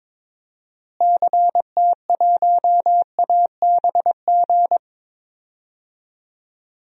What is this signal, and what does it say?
Morse code, heard as a keyed tone, about 22 wpm, 700 Hz. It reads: CT1ABG
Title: DXCC-100 Call Signs - 6 Characters - Repeated 2x - 22wpm